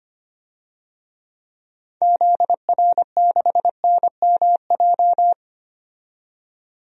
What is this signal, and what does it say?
Morse code, heard as a keyed tone, about 25 wpm, 690 Hz. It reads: ZR6NMJ
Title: DXCC-100 Call Signs - 6 Characters - 25wpm